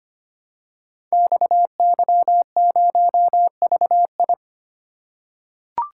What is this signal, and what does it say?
Morse code, heard as a keyed tone, about 25 wpm, 700 Hz. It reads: XY0VI
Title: Code Groups - 5 Letter:Numbers - 25wpm